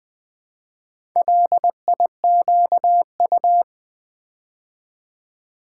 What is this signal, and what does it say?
Morse code, heard as a keyed tone, about 20 wpm, 690 Hz. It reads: LIQU
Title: Build A Word - 2 Letters At A Time - 6-Letter Words - 20wpm